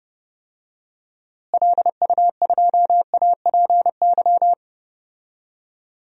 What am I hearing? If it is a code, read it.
Morse code, heard as a keyed tone, about 30 wpm, 700 Hz. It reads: LU2APY
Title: DXCC-100 Call Signs - 6 Characters - Repeated 2x - 30wpm